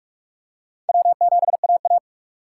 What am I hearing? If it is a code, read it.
Morse code, heard as a keyed tone, about 46 wpm, 700 Hz. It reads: W7AA
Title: USA Call Signs - 4 Characters - 4x Character Spacing - 45wpm